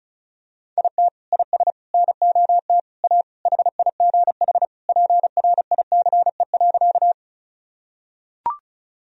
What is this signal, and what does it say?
Morse code, heard as a keyed tone, about 34 wpm, 700 Hz. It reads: IT IS NOT A HIGH PRICE.
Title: Sentences from Top 1000 Words - Review Part 4 - 35wpm